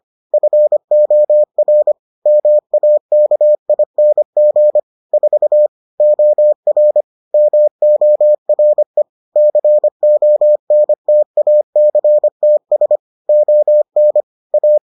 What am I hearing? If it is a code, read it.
Morse code, heard as a keyed tone, about 25 wpm, 600 Hz. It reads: FOR MAKING 4 OR MORE CONTACTS ON A